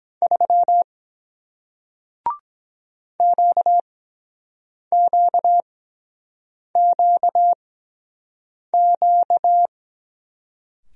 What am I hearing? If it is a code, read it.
Morse code, heard as a keyed tone, about 22 wpm, spaced as for 7 wpm, 700 Hz. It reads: 3 QQQQ